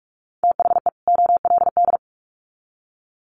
Morse code, heard as a keyed tone, about 44 wpm, 700 Hz. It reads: THE OLD
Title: Sets of 2 Words 45wpm